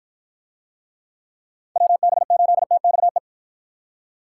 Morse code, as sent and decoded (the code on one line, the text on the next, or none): .-- -... ---.. - -..- .
WB8TXE